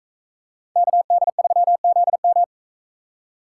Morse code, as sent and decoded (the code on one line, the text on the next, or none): -.- -.. ...-- --.. --
KD3ZM